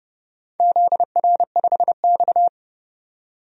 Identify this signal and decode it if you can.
Morse code, heard as a keyed tone, about 30 wpm, 700 Hz. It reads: ZR5X
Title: DXCC-100 Call Signs - 4 Characters - 30wpm